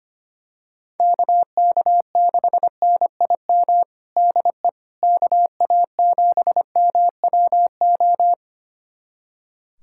Morse code, heard as a keyed tone, about 25 wpm, 700 Hz. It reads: KK6NIM DE KA7MWO
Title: USA Callsign DE Callsign - 6 Characters - 25wpm